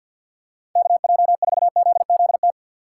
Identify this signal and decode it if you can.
Morse code, heard as a keyed tone, about 51 wpm, 700 Hz. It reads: KJ4ZZT
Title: USA Call Signs - 6 Characters - 4x Character Spacing - 50wpm